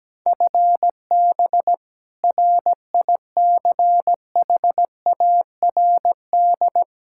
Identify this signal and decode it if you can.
Morse code, heard as a keyed tone, about 17 wpm, 700 Hz. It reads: FB RICHARD
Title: Practice QSO - Exchange 3 - More Info - 17wpm